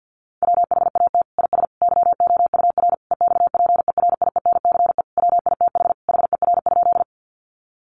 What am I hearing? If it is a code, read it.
Morse code, heard as a keyed tone, about 49 wpm, 700 Hz. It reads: WHAT IS YOUR EXPERIENCE WITH HER?
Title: Sentences from Top 600 Words 50wpm